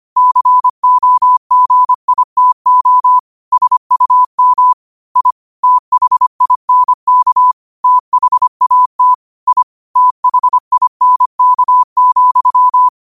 Morse code, heard as a keyed tone, about 25 wpm, 1000 Hz. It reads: COGITO SUM I THINK THAT I THINK,